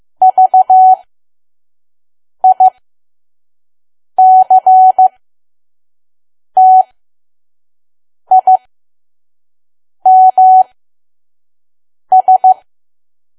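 Morse code, heard as a keyed tone, about 15 wpm, spaced as for 5 wpm, 750 Hz. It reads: VICTIMS
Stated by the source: Morse code practice at 5 WPM from KQ2P.COM.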